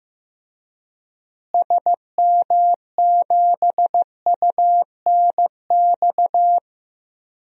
Morse code, heard as a keyed tone, about 15 wpm, 700 Hz. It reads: SM7UNX
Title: DXCC-100 Call Signs - 6 Characters - Repeated 3x - 15wpm